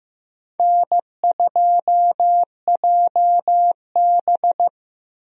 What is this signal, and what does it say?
Morse code, heard as a keyed tone, about 15 wpm, 700 Hz. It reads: N2JB
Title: USA Call Signs - 4 Characters - 15wpm